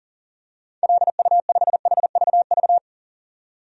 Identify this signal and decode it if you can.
Morse code, heard as a keyed tone, about 40 wpm, 700 Hz. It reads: LU5HVV
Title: DXCC-100 Call Signs - 6 Characters - 40wpm